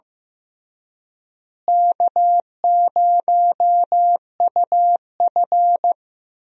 Morse code, heard as a keyed tone, about 15 wpm, 700 Hz. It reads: K0UF
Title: USA Call Signs - 4 Characters - Repeated 3x - 15wpm